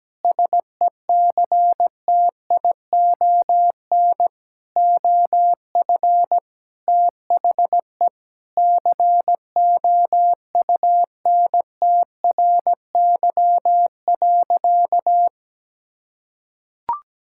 Morse code, heard as a keyed tone, about 17 wpm, 700 Hz. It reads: SECTION OF THE COUNTRY.
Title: Sentences from Top 1000 Words - Review Part 5 - 17wpm